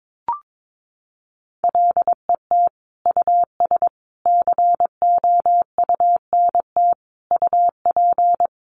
Morse code, heard as a keyed tone, about 22 wpm, 700 Hz. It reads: LET US COUNT UP
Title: Sentences from Top 1500 Words 22wpm